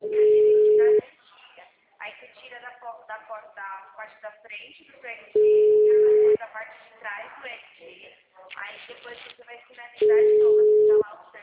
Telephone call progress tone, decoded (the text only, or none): ringback tone